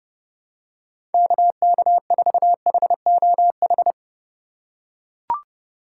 Morse code, heard as a keyed tone, about 30 wpm, 700 Hz. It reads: KK4HOH
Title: USA Call Signs - 6 Characters - 2x Character Spacing - 30wpm